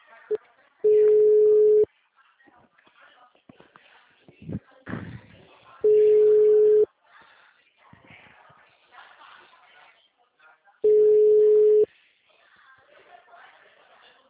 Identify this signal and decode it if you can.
Ringback tone (Europe, 425 Hz), with speech.
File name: out- -2011- - - .6142